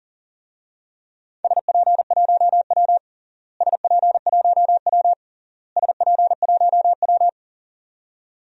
Morse code, heard as a keyed tone, about 40 wpm, 700 Hz. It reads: SP1W SP1W SP1W
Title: DXCC-100 Call Signs - 4 Characters - Repeated 3x - 40wpm